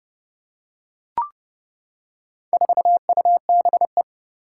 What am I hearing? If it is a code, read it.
Morse code, heard as a keyed tone, about 30 wpm, 700 Hz. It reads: 4UBE
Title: Four Challenging Letter-Number Combinations 30wpm